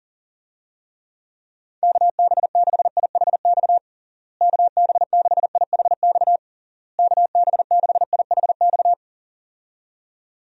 Morse code, heard as a keyed tone, about 40 wpm, 700 Hz. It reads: KB6IHX KB6IHX KB6IHX
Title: USA Call Signs - 6 Characters - Repeated 3x - 40wpm